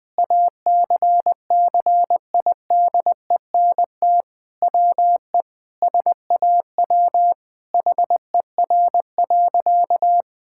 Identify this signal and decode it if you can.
Morse code, heard as a keyed tone, about 20 wpm, 700 Hz. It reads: ACCIDENT WE SAW HER.
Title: Sentences with 4-Word Phrases - 20wpm